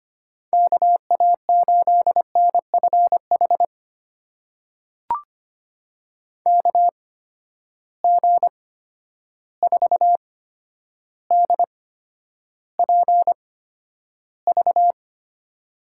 Morse code, heard as a keyed tone, about 25 wpm, spaced as for 11 wpm, 700 Hz. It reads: KA8NFH KG4DPV
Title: USA Call Signs - 6 Characters - 8x Character Spacing - 25wpm